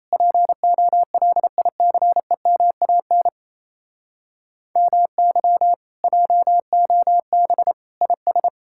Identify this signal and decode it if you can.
Morse code, heard as a keyed tone, about 38 wpm, 700 Hz. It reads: POLICEMAN MY JOB IS